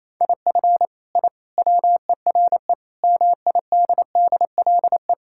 Morse code, heard as a keyed tone, about 28 wpm, 700 Hz. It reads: IF I WERE MIDDLE
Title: Sentences from Top 1000 Words - Review Part 6 - 28wpm